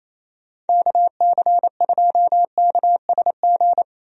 Morse code, heard as a keyed tone, about 28 wpm, 700 Hz. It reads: KC2KSG
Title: USA Call Signs - 6 Characters - 28wpm